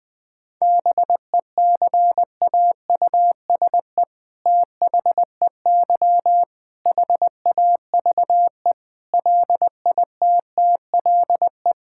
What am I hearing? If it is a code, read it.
Morse code, heard as a keyed tone, about 21 wpm, 700 Hz. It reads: BECAUSE THEY HAVE LITTLE
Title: All-Age Appropriate Jokes - 20wpm